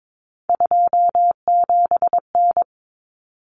Morse code, heard as a keyed tone, about 22 wpm, 700 Hz. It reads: 27N
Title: Common Cut Numbers - 3-Digit 22wpm